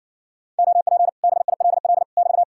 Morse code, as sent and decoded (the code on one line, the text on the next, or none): --- .--. / -... .. .-.. .-.. / -...-
OP BILL <BT>